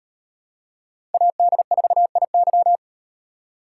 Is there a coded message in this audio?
Yes: Morse code from 1.1 to 2.8 s, heard as a keyed tone, about 35 wpm, 700 Hz. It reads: AD4IY